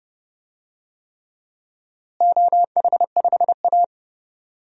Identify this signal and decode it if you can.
Morse code, heard as a keyed tone, about 32 wpm, 700 Hz. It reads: OH5A
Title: DXCC-100 Call Signs - 4 Characters - Repeated 3x - 30wpm